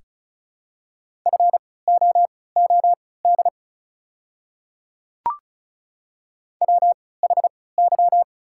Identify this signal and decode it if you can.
Morse code, heard as a keyed tone, about 34 wpm, spaced as for 20 wpm, 690 Hz. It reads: FOOD WHY
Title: 3x Character Spacing - Top 500 Words - 35wpm